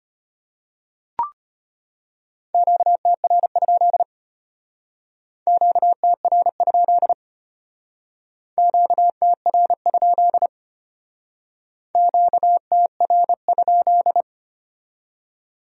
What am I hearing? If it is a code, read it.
Morse code, heard as a keyed tone, about 31 wpm, 700 Hz. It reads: QTR? QTR? QTR? QTR?
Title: Speed-Racing Q-Codes - 25wpm